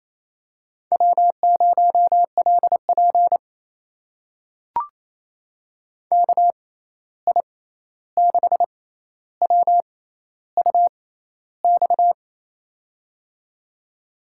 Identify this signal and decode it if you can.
Morse code, heard as a keyed tone, about 28 wpm, spaced as for 13 wpm, 700 Hz. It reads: W0LP KI6WUX
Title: USA Call Signs - 6x Character Spacing - 28wpm